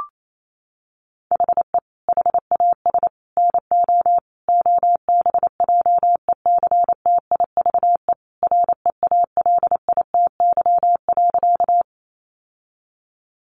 Morse code, heard as a keyed tone, about 28 wpm, 700 Hz. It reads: HE HAS NO OBJECTIVE REALITY.
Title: Sentences from Top 1200 Words 28wpm